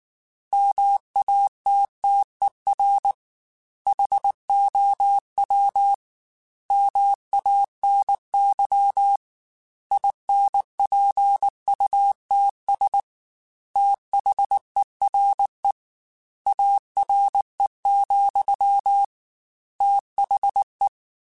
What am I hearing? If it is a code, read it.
Morse code, heard as a keyed tone, about 19 wpm, 800 Hz. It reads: MATTER HOW MANY INPUTS THERE ARE, THE
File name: Technical Subjects / Technical Subjects-mp3s / OR Gate @18